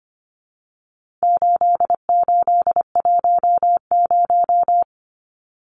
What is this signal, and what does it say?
Morse code, heard as a keyed tone, about 25 wpm, 700 Hz. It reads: 8810
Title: Numbers - 4-Digit 25wpm